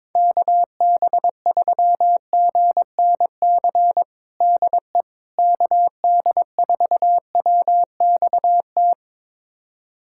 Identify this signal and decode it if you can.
Morse code, heard as a keyed tone, about 22 wpm, 700 Hz. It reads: KB3GNC DE KD4WXT
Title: USA Callsign DE Callsign - 6 Characters - 22wpm